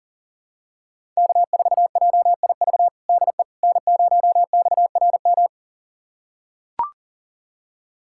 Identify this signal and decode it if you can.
Morse code, heard as a keyed tone, about 40 wpm, 700 Hz. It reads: K4JIV DE N0XRM
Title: USA Callsign DE Callsign - 5 Characters - 40wpm